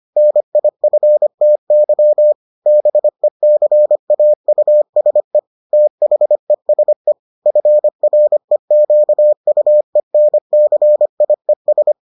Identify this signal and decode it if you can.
Morse code, heard as a keyed tone, about 25 wpm, 600 Hz. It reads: NIFTY BECAUSE THESE FREQUENCIES